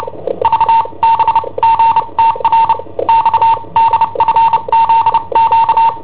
Morse code, heard as a keyed tone, about 29 wpm, spaced as for 24 wpm, 910 Hz. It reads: VBGTR XDFZQ
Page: Cassetta esercizi preparata da I7OHP per IW5ELP